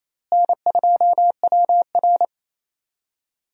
Morse code, heard as a keyed tone, about 28 wpm, 700 Hz. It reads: N2WR
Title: USA Call Signs - 4 Characters - Repeated 3x - 28wpm